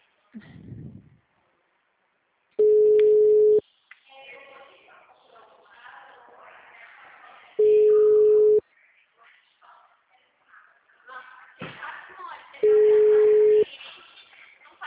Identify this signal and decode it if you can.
Ringback tone (Europe, 425 Hz), with speech.